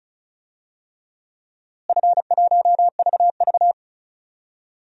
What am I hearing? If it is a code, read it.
Morse code, heard as a keyed tone, about 34 wpm, 700 Hz. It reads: F1VV
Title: DXCC-100 Call Signs - 4 Characters - Repeated 3x - 35wpm